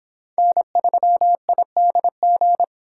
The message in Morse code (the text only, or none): N3IDG